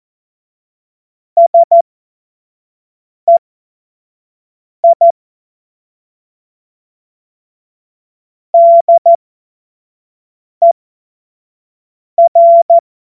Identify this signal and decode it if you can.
Morse code, heard as a keyed tone, about 14 wpm, spaced as for 5 wpm, 680 Hz. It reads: SEI DER
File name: Morsekurs / Witze / Witz 15. November 2022 25 F70